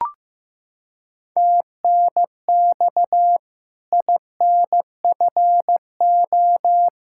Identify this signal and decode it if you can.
Morse code, heard as a keyed tone, about 16 wpm, 700 Hz. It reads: TNX INFO
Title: Practice QSO - Exchange 4 - Ending - 15wpm